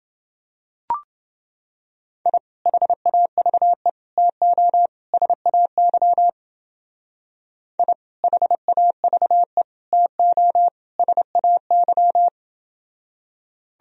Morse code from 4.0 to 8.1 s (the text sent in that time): TO SAY I